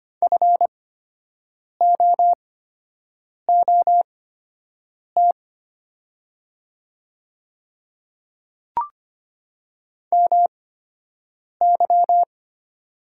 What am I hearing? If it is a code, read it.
Morse code, heard as a keyed tone, about 25 wpm, spaced as for 7 wpm, 700 Hz. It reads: FOOT MY
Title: 8x Character Spacing - Top 500 Words - 25wpm